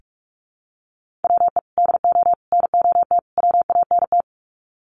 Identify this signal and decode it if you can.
Morse code, heard as a keyed tone, about 44 wpm, 700 Hz. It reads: WE DO NOT WANT
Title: Sets of 4 Words 45wpm